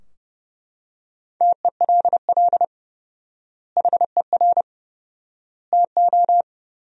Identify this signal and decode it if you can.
Morse code, heard as a keyed tone, about 32 wpm, 700 Hz. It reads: TELL HER TO
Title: 4x Word-Spacing - Sentences from Top 300 Words - 30wpm